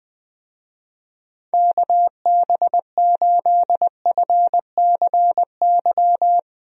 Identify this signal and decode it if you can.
Morse code, heard as a keyed tone, about 20 wpm, 700 Hz. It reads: KB8FCY
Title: USA Call Signs - 6 Characters - 20wpm